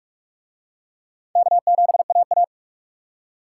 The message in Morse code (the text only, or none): K7AA